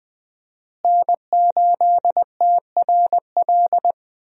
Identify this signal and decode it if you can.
Morse code, heard as a keyed tone, about 20 wpm, 700 Hz. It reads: N8TRL
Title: USA Call Signs - 5 Characters - Repeated 2x - 20wpm